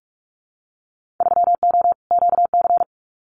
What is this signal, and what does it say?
Morse code, heard as a keyed tone, about 46 wpm, 700 Hz. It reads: 3O QC
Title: Winter Field Day Exchanges Encore 45wpm